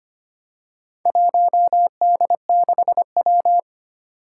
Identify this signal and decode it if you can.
Morse code, heard as a keyed tone, about 25 wpm, 700 Hz. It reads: 1D6W
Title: Four Challenging Letter-Number Combinations 25wpm